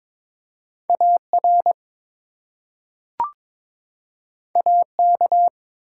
Morse code, heard as a keyed tone, about 22 wpm, 700 Hz. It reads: AR AK
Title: US States and Canadian Provinces 22wpm